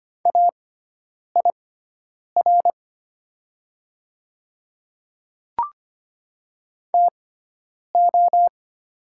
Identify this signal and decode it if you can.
Morse code, heard as a keyed tone, about 25 wpm, spaced as for 9 wpm, 700 Hz. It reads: AIR TO
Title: 6x Character Spacing - Top 200 Words - 25wpm